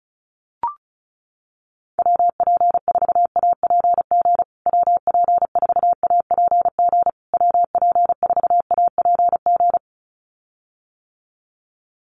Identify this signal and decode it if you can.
Morse code, heard as a keyed tone, about 34 wpm, 700 Hz. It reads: WP4APG WP4APG WP4APG
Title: USA Call Signs Repeated 3x 35wpm